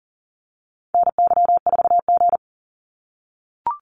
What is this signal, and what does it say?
Morse code, heard as a keyed tone, about 42 wpm, 700 Hz. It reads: NY4G
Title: NA SOTA Activator Callsigns 40wpm